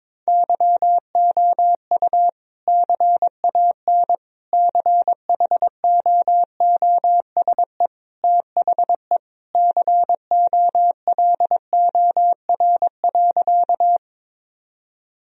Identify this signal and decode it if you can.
Morse code, heard as a keyed tone, about 22 wpm, 700 Hz. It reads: YOU CAN CHOOSE THE COLOR.
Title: Sentences from Top 1000 Words - Review Part 4 - 22wpm